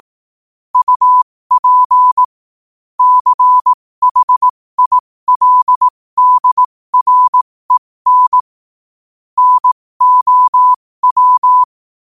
Morse code, heard as a keyed tone, about 18 wpm, spaced as for 15 wpm, 1000 Hz. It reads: UP CHILDREN NOW
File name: qotdfiles / 15wpm